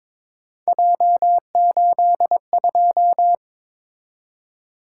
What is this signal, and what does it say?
Morse code, heard as a keyed tone, about 22 wpm, 700 Hz. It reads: J82
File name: practice / 22wpm / Three Letter-Number Combinations - 22wpm